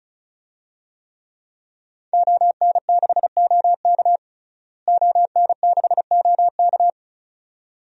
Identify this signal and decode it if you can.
Morse code, heard as a keyed tone, about 36 wpm, 700 Hz. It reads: ON6OK ON6OK
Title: DXCC-100 Call Signs - 5 Characters - Repeated 2x - 35wpm